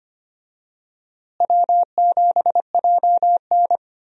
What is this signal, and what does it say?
Morse code, heard as a keyed tone, about 25 wpm, 700 Hz. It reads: W7JN
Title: USA Call Signs - 4 Characters - Repeated 3x - 25wpm